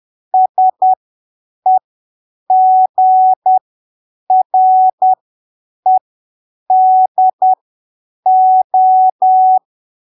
Morse code, heard as a keyed tone, about 10 wpm, spaced as for 7 wpm, 750 Hz. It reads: SEGREDO
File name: arquivos / texto / texto 20 10ppm